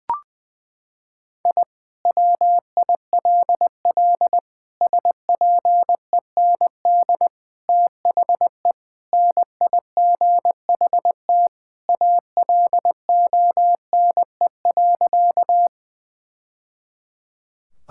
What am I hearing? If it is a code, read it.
Morse code, heard as a keyed tone, about 20 wpm, 700 Hz. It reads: I WILL SPEND THE NIGHT ALONE.
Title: Sentences from Top 1000 Words - Review Part 5 - 20wpm